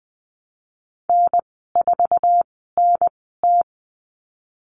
Morse code, heard as a keyed tone, about 20 wpm, spaced as for 15 wpm, 700 Hz. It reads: N4NT
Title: 055 Lesson 10g - Letter r - Call Signs 20wpm